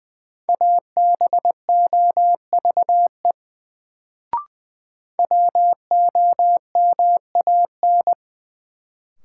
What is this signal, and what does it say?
Morse code, heard as a keyed tone, about 20 wpm, 700 Hz. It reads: ABOVE WOMAN
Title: Sentences from Top 1000 Words - Review Part 2 - New Word Drill - 20wpm